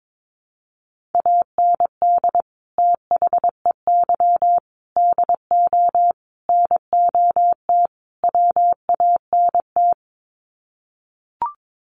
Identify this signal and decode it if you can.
Morse code, heard as a keyed tone, about 22 wpm, 700 Hz. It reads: AND THEY DO NOT WANT
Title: Sets of 5 Words Encore2 22wpm